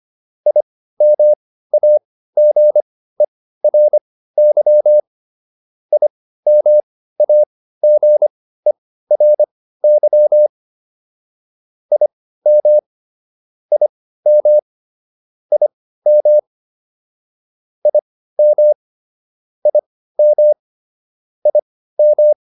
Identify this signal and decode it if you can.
Morse code, heard as a keyed tone, about 25 wpm, spaced as for 15 wpm, 600 Hz. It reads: IMAGERY IMAGERY IM IM IM IM IM IM